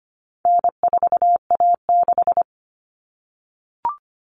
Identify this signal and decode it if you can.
Morse code, heard as a keyed tone, about 25 wpm, 700 Hz. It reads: N4A6
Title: Common Cut Numbers - 4-Digit 25wpm